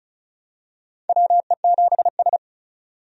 Morse code, heard as a keyed tone, about 34 wpm, 700 Hz. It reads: WE7S
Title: USA Call Signs - 4 Characters - 35wpm